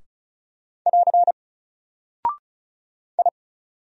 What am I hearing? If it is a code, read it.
Morse code, heard as a keyed tone, about 36 wpm, 700 Hz. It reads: <AR> I
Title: Single Character (Letter:Number:Punctuation:Prosign) 35wpm